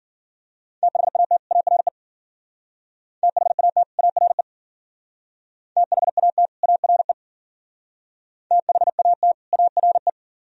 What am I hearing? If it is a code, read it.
Morse code, heard as a keyed tone, about 48 wpm, 700 Hz. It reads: THAT ARE THAT ARE THAT ARE THAT ARE
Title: Speed-Racing - Sets of 2 Words 40wpm